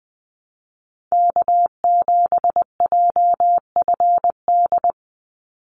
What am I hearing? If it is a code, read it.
Morse code, heard as a keyed tone, about 20 wpm, 700 Hz. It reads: K7JFD
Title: NA SOTA Activator Callsigns 20wpm